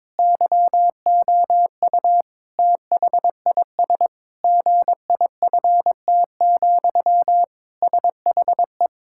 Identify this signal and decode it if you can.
Morse code, heard as a keyed tone, about 22 wpm, 700 Hz. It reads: YOU THIS GIFT, SHE
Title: Sentences from Top 1000 Words - Review Part 6 - 22wpm